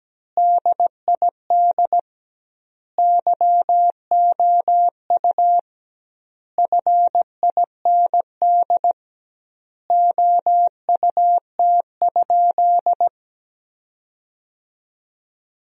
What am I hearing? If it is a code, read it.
Morse code, heard as a keyed tone, about 17 wpm, 700 Hz. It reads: DID YOU FIND OUT?
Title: 2x Word-Spacing - Sentences from Top 100 Words - 17wpm